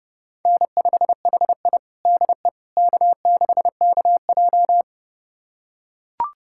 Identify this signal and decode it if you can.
Morse code, heard as a keyed tone, about 30 wpm, 700 Hz. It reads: N5HI DE K6KJ
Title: USA Callsign DE Callsign - 4 Characters - 30wpm